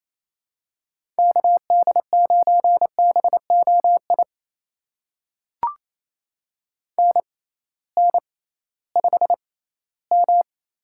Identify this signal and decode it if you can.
Morse code, heard as a keyed tone, about 28 wpm, spaced as for 16 wpm, 700 Hz. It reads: KD9BOI NN5M
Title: USA Call Signs - 6x Character Spacing - 28wpm